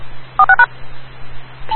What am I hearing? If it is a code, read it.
DTMF, heard as telephone touch tones, tones of about 90 ms. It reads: 46#
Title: Rocky Mountain Radio League Repeaters